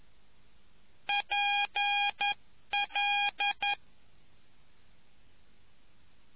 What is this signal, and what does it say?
Morse code, heard as a keyed tone, about 11 wpm, 2450 Hz. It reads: PL